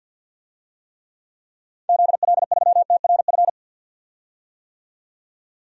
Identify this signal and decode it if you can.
Morse code, heard as a keyed tone, about 46 wpm, 700 Hz. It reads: ZL3TRF
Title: DXCC-100 Call Signs - 6 Characters - 50wpm